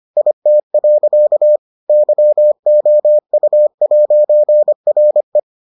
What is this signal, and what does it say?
Morse code, heard as a keyed tone, about 25 wpm, 600 Hz. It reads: IT. YOU'RE